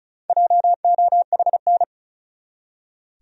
Morse code, heard as a keyed tone, about 34 wpm, 700 Hz. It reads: JOHN
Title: USA Common Names - Max 5 Letters - 35wpm